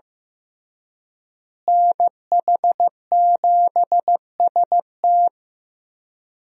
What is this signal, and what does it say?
Morse code, heard as a keyed tone, about 15 wpm, 700 Hz. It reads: NH7ST
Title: USA Call Signs - 5 Characters - Repeated 3x - 15wpm